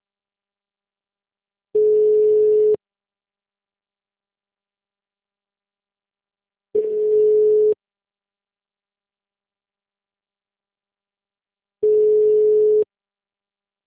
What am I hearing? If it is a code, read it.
Ringback tone (Europe, 425 Hz).